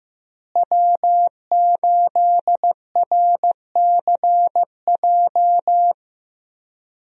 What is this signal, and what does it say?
Morse code, heard as a keyed tone, about 15 wpm, 700 Hz. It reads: W8RCJ
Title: USA Call Signs - 5 Characters - Repeated 3x - 15wpm